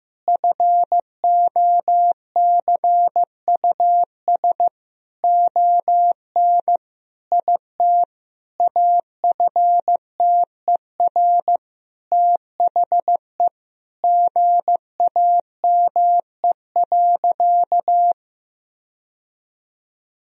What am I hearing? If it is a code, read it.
Morse code, heard as a keyed tone, about 14 wpm, 700 Hz. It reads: FOCUS ON IT AFTER THE GAME.
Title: Sentences from Top 1000 Words - Review Part 4 - 15wpm